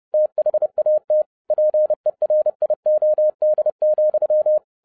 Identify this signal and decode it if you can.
Morse code, heard as a keyed tone, about 30 wpm, 610 Hz. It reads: THAT PERIOD,
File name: 2012-01-mp3 / n11-I DIDNT MISS THE SOLAR CYCLE 24-30wpm